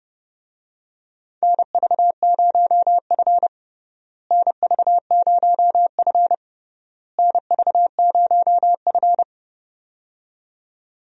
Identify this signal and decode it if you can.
Morse code, heard as a keyed tone, about 30 wpm, 700 Hz. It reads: NV0F NV0F NV0F
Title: USA Call Signs - 4 Characters - Repeated 3x - 30wpm